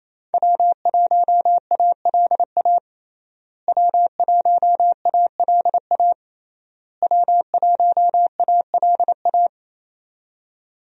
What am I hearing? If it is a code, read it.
Morse code, heard as a keyed tone, about 28 wpm, 700 Hz. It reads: W1ALA W1ALA W1ALA
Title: USA Call Signs - 5 Characters - Repeated 3x - 28wpm